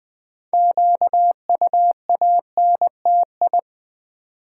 Morse code, heard as a keyed tone, about 20 wpm, 700 Hz. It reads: QUANTI
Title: Build A Word - 2 Letters At A Time - 8-Letter Words - 20wpm